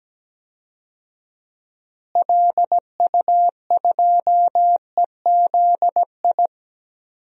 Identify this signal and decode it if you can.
Morse code, heard as a keyed tone, about 17 wpm, 700 Hz. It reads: LU2EZI
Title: DXCC-100 Call Signs - 6 Characters - Repeated 2x - 17wpm